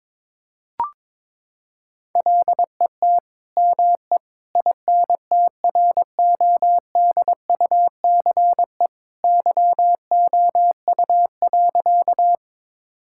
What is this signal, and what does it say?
Morse code, heard as a keyed tone, about 22 wpm, 700 Hz. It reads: LET ME INTRODUCE YOU.
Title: Sentences from Top 1600 Words 22wpm